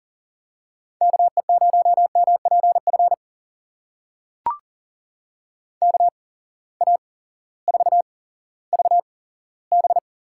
Morse code, heard as a keyed tone, about 40 wpm, spaced as for 18 wpm, 700 Hz. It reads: KE0MPF KA4VB
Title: USA Call Signs - 6 Characters - 8x Character Spacing - 40wpm